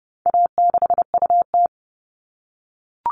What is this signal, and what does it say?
Morse code, heard as a keyed tone, about 30 wpm, 700 Hz. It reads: A6UT
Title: All Cut Numbers - 4-Digit 30wpm